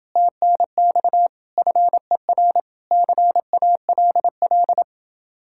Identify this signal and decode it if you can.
Morse code, heard as a keyed tone, about 30 wpm, 700 Hz. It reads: TNX FER CALL